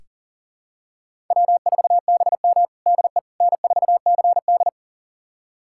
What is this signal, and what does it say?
Morse code, heard as a keyed tone, about 40 wpm, 700 Hz. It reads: W4BM DE N4CD
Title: USA Callsign DE Callsign - 4 Characters - 40wpm